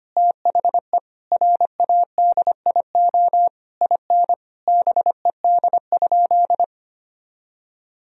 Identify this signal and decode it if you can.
Morse code, heard as a keyed tone, about 25 wpm, 700 Hz. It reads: THE RADIO IN BED?
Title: Sentences from Top 1000 Words - Review Part 3 - 25wpm